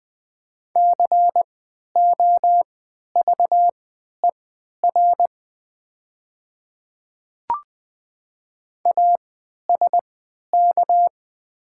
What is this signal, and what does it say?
Morse code, heard as a keyed tone, about 20 wpm, spaced as for 11 wpm, 700 Hz. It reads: COVER ASK
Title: 3x Character Spacing - Top 300 Words - 20wpm